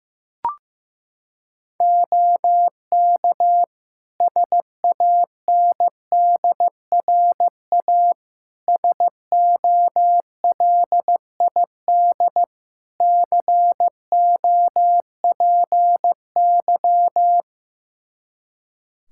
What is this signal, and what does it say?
Morse code, heard as a keyed tone, about 16 wpm, 700 Hz. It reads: OK SANDRA SOLID COPY
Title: Practice QSO - Exchange 3 - More Info - 15wpm